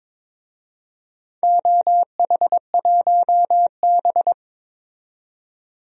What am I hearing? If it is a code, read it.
Morse code, heard as a keyed tone, about 22 wpm, 700 Hz. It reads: OH1B
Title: DXCC-100 Call Signs - 4 Characters - 22wpm